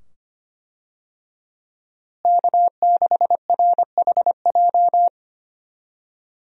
Morse code, heard as a keyed tone, about 25 wpm, 700 Hz. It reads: K6RHJ
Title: USA Call Signs - 5 Characters - Repeated 2x - 25wpm